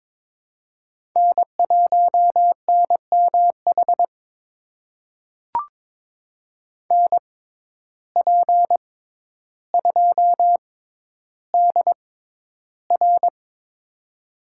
Morse code, heard as a keyed tone, about 22 wpm, spaced as for 11 wpm, 700 Hz. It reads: N1NMH NP2DR
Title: USA Call Signs - 5 Characters - 6x Character Spacing - 22wpm